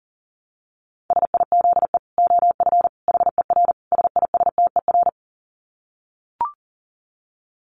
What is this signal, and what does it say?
Morse code, heard as a keyed tone, about 40 wpm, 700 Hz. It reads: SIZE OF HER SISTER
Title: Dit Challenge 40wpm